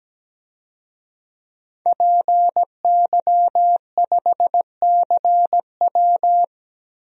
Morse code, heard as a keyed tone, about 17 wpm, 700 Hz. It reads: PY5CW
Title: DXCC-100 Call Signs - 5 Characters - 17wpm